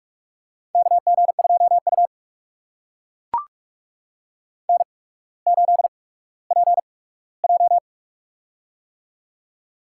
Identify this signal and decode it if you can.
Morse code, heard as a keyed tone, about 41 wpm, spaced as for 19 wpm, 700 Hz. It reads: KG2U N8PJ
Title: USA Call Signs - 4 Characters - 8x Character Spacing - 45wpm